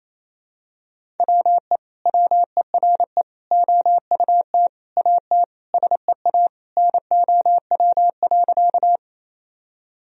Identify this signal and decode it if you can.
Morse code, heard as a keyed tone, about 28 wpm, 700 Hz. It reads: WE WERE OUT AT SEA NOW.
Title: Sentences from Top 1000 Words - Review Part 1 - 28wpm